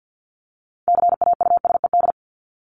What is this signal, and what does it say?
Morse code, heard as a keyed tone, about 50 wpm, 700 Hz. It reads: CAUSED
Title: ED Suffix - Top 100 Words with Base Words 50wpm